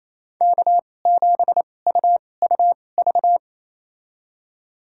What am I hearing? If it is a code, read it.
Morse code, heard as a keyed tone, about 28 wpm, spaced as for 20 wpm, 700 Hz. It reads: K7UUV
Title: USA Call Signs - 5 Characters - 2x Character Spacing - 28wpm